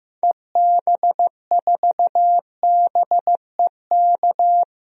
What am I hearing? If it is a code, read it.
Morse code, heard as a keyed tone, about 15 wpm, 700 Hz. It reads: EB4BEK
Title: DXCC-100 Call Signs - 6 Characters - Repeated 3x - 15wpm